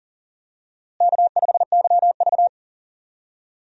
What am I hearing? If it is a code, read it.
Morse code, heard as a keyed tone, about 38 wpm, 700 Hz. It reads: K5YV
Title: USA Call Signs - 4 Characters - Repeated 2x - 40wpm